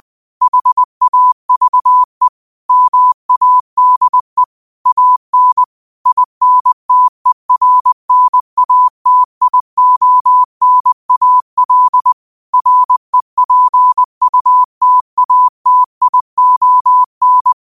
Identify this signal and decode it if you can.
Morse code, heard as a keyed tone, about 20 wpm, 1000 Hz. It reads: HAVE MADE AN INTERNATIONAL REPUTATION
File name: qotdfiles / 20wpm